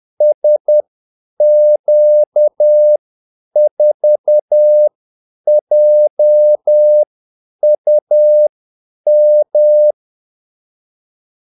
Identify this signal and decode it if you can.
Morse code, heard as a keyed tone, about 10 wpm, spaced as for 8 wpm, 600 Hz. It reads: SQ4JUM
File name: Callsigns-10-8wpm